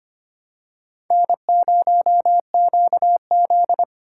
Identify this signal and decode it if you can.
Morse code, heard as a keyed tone, about 25 wpm, 700 Hz. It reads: N0QZ
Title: USA Call Signs - 4 Characters - 3x Character Spacing - 25wpm